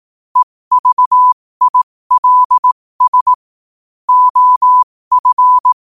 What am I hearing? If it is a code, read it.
Morse code, heard as a keyed tone, about 18 wpm, spaced as for 15 wpm, 1000 Hz. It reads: EVILS OF
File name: qotdfiles / 15wpm